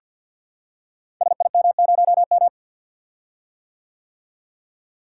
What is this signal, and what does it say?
Morse code, heard as a keyed tone, about 51 wpm, 700 Hz. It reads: SIM0M
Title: Code Groups - 5 Letter:Numbers - 50wpm